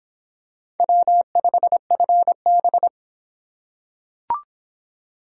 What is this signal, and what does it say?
Morse code, heard as a keyed tone, about 25 wpm, 700 Hz. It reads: W5FB